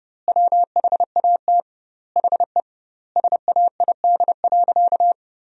Morse code, heard as a keyed tone, about 30 wpm, 700 Hz. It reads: WHAT HE SAID.
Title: 2x Word-Spacing - Sentences from Top 400 Words - 30wpm